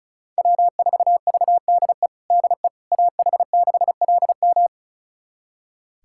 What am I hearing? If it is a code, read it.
Morse code, heard as a keyed tone, about 36 wpm, 700 Hz. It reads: W4VDE DE AH6LM
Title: USA Callsign DE Callsign - 5 Characters - 35wpm